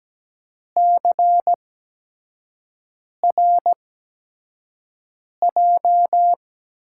Morse code, heard as a keyed tone, about 17 wpm, spaced as for 5 wpm, 700 Hz. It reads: CRJ
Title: USA Call Signs - 8x Character Spacing - 17wpm